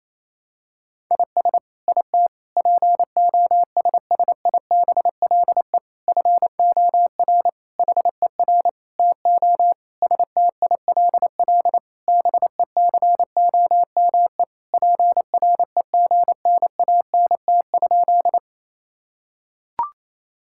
Morse code, heard as a keyed tone, about 28 wpm, 700 Hz. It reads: IS IT POSSIBLE FOR HER TO STILL BECOME PREGNANT?
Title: Sentences from Top 1000 Words - Review Part 6 - 28wpm